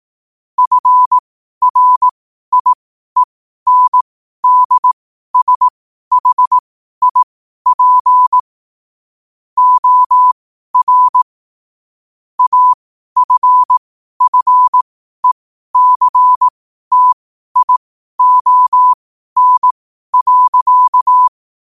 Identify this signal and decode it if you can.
Morse code, heard as a keyed tone, about 18 wpm, spaced as for 12 wpm, 1000 Hz. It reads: FRIENDSHIP OR AFFECTION.
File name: qotdfiles / 12wpm